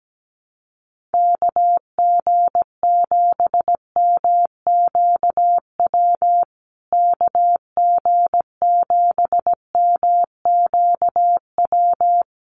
Morse code, heard as a keyed tone, about 17 wpm, 700 Hz. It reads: KG7MQW KG7MQW
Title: USA Call Signs Repeated 2x 17wpm